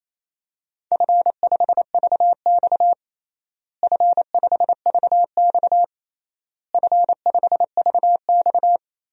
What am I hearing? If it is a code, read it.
Morse code, heard as a keyed tone, about 27 wpm, 700 Hz. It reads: F5VX F5VX F5VX
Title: DXCC-100 Call Signs - 4 Characters - Repeated 3x - 28wpm